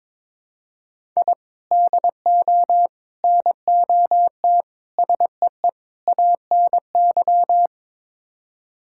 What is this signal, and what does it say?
Morse code, heard as a keyed tone, about 22 wpm, 700 Hz. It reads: I DO NOT SEE ANY
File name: practice / 22wpm / Build A Phrase - 5-Word Phrases - 22wpm